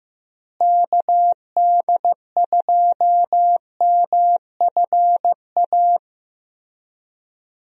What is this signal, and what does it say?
Morse code, heard as a keyed tone, about 15 wpm, 700 Hz. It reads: KD2MFA
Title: USA Call Signs - 6 Characters - Repeated 3x - 15wpm